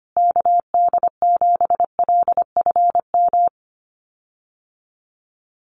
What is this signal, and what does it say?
Morse code, heard as a keyed tone, about 25 wpm, 700 Hz. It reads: KD7LFM
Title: USA Call Signs Repeated 2x 25wpm